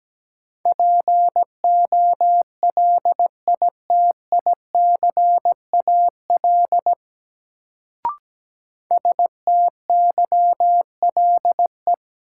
Morse code, heard as a keyed tone, about 17 wpm, 700 Hz. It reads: POLITICAL STYLE
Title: Sentences from Top 800 Words - New Word Drill - 17wpm